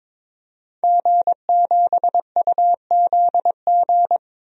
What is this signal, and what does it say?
Morse code, heard as a keyed tone, about 22 wpm, 700 Hz. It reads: G7UZG
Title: DXCC-100 Call Signs - 5 Characters - 22wpm